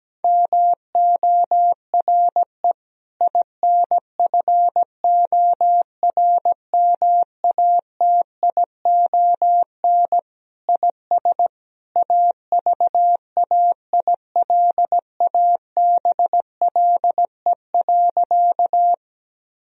Morse code, heard as a keyed tone, about 17 wpm, 700 Hz. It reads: MORE INFORMATION IS AVAILABLE.
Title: Sentences from Top 1000 Words - Review Part 3 - 17wpm